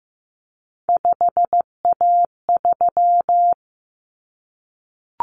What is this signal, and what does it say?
Morse code, heard as a keyed tone, about 15 wpm, 700 Hz. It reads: 5A3
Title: Common Cut Numbers - 3-Digit 15wpm